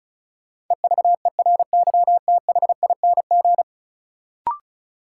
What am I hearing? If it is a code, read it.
Morse code, heard as a keyed tone, about 36 wpm, 700 Hz. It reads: EVERYTHING
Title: Sentences from Top 700 Words - New Word Drill - 35wpm